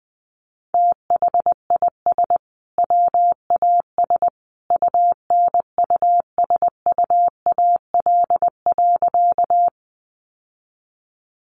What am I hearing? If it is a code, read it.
Morse code, heard as a keyed tone, about 19 wpm, 700 Hz. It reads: THIS WAS UNUSUAL.
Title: Sentences from Top 1400 Words 20wpm